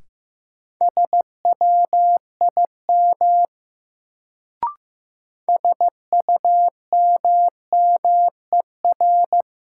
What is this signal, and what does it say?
Morse code, heard as a keyed tone, about 15 wpm, 700 Hz. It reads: SWIM SUMMER
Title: Sentences from Top 1000 Words - Review Part 6 - New Word Drill - 15wpm